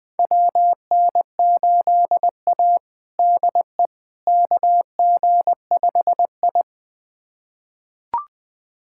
Morse code, heard as a keyed tone, about 20 wpm, 700 Hz. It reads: WN8A DE KG5I
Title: USA Callsign DE Callsign - 4 Characters - 20wpm